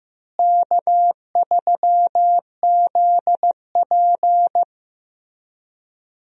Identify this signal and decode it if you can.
Morse code, heard as a keyed tone, about 15 wpm, 700 Hz. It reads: K3ZP
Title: USA Call Signs - 4 Characters - Repeated 3x - 15wpm